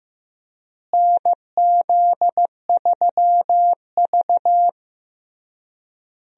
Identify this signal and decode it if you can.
Morse code, heard as a keyed tone, about 15 wpm, 700 Hz. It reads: NZ3V
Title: USA Call Signs - 4 Characters - Repeated 3x - 15wpm